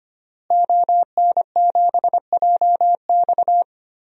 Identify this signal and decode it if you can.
Morse code, heard as a keyed tone, about 25 wpm, 700 Hz. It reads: ON7JX
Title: DXCC-100 Call Signs - 5 Characters - Repeated 3x - 25wpm